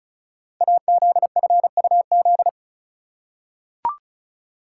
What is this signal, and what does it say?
Morse code, heard as a keyed tone, about 36 wpm, 700 Hz. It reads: AZFUZ
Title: Code Groups - 5 Letters - 35wpm